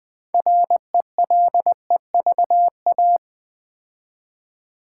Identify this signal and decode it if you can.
Morse code, heard as a keyed tone, about 20 wpm, 700 Hz. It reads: RELEVA
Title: Build A Word - 3 Letters At A Time - 9-Letter Words - 20wpm